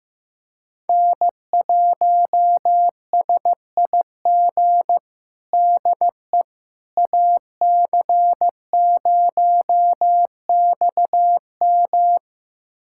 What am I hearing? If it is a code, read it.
Morse code, heard as a keyed tone, about 15 wpm, 700 Hz. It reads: N1SIG DE AC0XM
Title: USA Callsign DE Callsign - 5 Characters - 15wpm